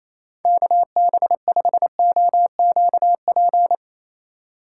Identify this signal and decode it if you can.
Morse code, heard as a keyed tone, about 28 wpm, 700 Hz. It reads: KB5OQP
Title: USA Call Signs - 3x Character Spacing - 28wpm